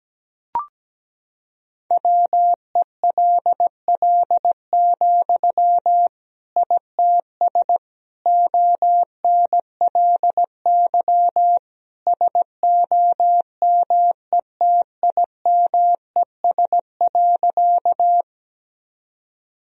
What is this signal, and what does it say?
Morse code, heard as a keyed tone, about 17 wpm, 700 Hz. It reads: WELL, ITS ONLY SOMETIMES.
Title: Sentences from Top 1000 Words - Review Part 3 - 17wpm